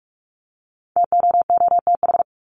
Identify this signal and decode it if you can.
Morse code, heard as a keyed tone, about 44 wpm, 700 Hz. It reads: TOOTH
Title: Top 1751-2000 Words 45wpm